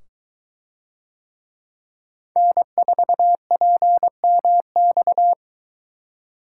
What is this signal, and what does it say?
Morse code, heard as a keyed tone, about 23 wpm, 700 Hz. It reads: N4PMX